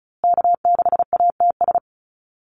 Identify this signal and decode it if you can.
Morse code, heard as a keyed tone, about 34 wpm, 700 Hz. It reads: K6ATS
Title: USA Call Signs Repeated 2x 35wpm